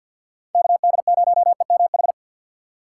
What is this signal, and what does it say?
Morse code, heard as a keyed tone, about 48 wpm, 700 Hz. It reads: KD0EMH
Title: USA Call Signs - 6 Characters - 3x Character Spacing - 50wpm